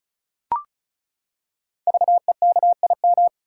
Morse code, heard as a keyed tone, about 36 wpm, 690 Hz. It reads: VEKIM
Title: Code Groups - 5 Letters - 35wpm